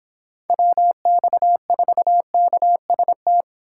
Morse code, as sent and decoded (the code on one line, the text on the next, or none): .-- -..- ....- -.- ... -
WX4KST